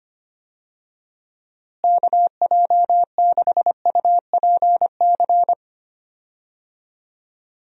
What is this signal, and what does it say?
Morse code, heard as a keyed tone, about 25 wpm, 700 Hz. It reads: KJ6UPC
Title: USA Call Signs - 6 Characters - Repeated 2x - 25wpm